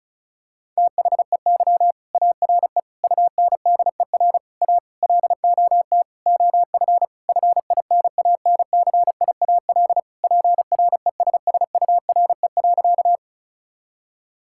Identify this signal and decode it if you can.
Morse code, heard as a keyed tone, about 36 wpm, 700 Hz. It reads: THEY ARE UNDER A LOT OF FINANCIAL PRESSURE.
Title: Sentences from Top 1000 Words - Review Part 6 - 35wpm